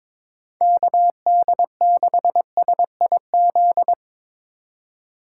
Morse code, heard as a keyed tone, about 21 wpm, 700 Hz. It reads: KD6SIZ
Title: USA Call Signs - 6 Characters - 2x Character Spacing - 22wpm